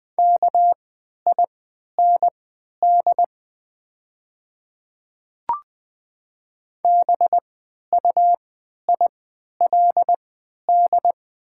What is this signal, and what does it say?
Morse code, heard as a keyed tone, about 20 wpm, spaced as for 11 wpm, 700 Hz. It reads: KIND BUILD
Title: 3x Character Spacing - Top 200 Words - 20wpm